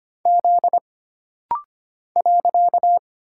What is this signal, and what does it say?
Morse code, heard as a keyed tone, about 25 wpm, 700 Hz. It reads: Z .
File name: Single Character (Letter:Number:Punctuation:Prosign) - Rapid-Fire - 25wpm